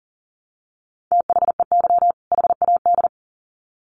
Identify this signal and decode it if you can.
Morse code, heard as a keyed tone, about 40 wpm, 700 Hz. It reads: THEY HAD
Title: Sets of 2 Words 40wpm